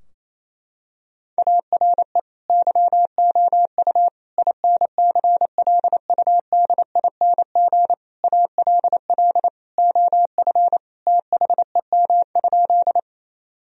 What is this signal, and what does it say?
Morse code, heard as a keyed tone, about 28 wpm, 700 Hz. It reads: ARE YOU INCLUDING ALL OF THEM?
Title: Sentences from Top 1000 Words - Review Part 4 - 28wpm